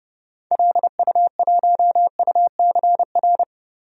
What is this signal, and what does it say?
Morse code, heard as a keyed tone, about 30 wpm, 700 Hz. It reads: LU1UCR
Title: DXCC-100 Call Signs - 6 Characters - 30wpm